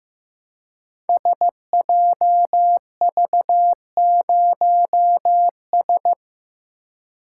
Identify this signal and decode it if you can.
Morse code, heard as a keyed tone, about 15 wpm, 700 Hz. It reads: SJV0S
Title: Code Groups - 5 Letter:Numbers - 15wpm